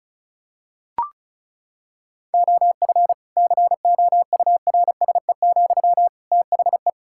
Morse code, heard as a keyed tone, about 36 wpm, 700 Hz. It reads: OF COURSE, THE
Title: Sentences from Top 1600 Words 35wpm